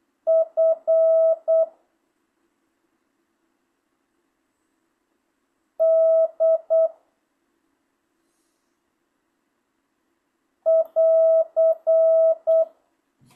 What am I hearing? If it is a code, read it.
Morse code, heard as a keyed tone, about 8 wpm, 660 Hz. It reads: F D <AR>
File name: jh8cbh / 2025cw